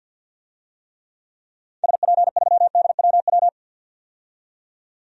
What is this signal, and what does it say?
Morse code, heard as a keyed tone, about 47 wpm, 700 Hz. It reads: SP3DWW
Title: DXCC-100 Call Signs - 6 Characters - 50wpm